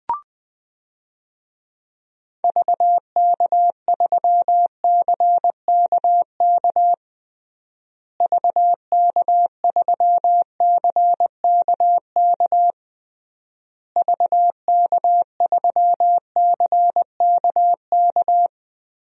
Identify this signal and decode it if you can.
Morse code, heard as a keyed tone, about 21 wpm, 700 Hz. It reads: VK3CKK VK3CKK VK3CKK
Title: DXCC-100 Call Signs - 6 Characters - Repeated 3x - 20wpm